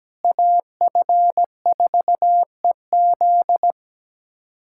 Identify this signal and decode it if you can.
Morse code, heard as a keyed tone, about 17 wpm, 700 Hz. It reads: AF4EZ
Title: USA Callsign DE Callsign - 5 Characters - 17wpm